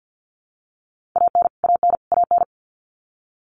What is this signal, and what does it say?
Morse code, heard as a keyed tone, about 50 wpm, 700 Hz. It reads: AN AN AN
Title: Top 500 Words Repeated 50wpm